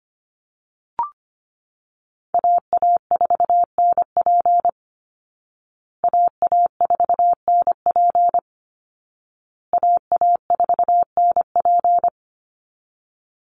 Morse code, heard as a keyed tone, about 25 wpm, 700 Hz. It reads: AA4NP AA4NP AA4NP
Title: Call Signs Repeated 2x - 25wpm